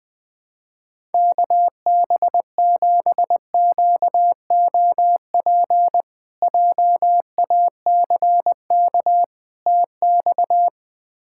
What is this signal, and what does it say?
Morse code, heard as a keyed tone, about 20 wpm, 700 Hz. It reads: KB7QOP JACK TX
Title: K1USN SST QSO Practice - 20wpm